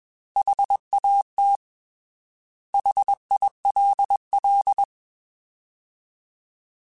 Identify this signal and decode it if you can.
Morse code, heard as a keyed tone, about 21 wpm, 800 Hz. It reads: HAT HILL
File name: Alphabetical / Alphabetical mp3s / Hs @20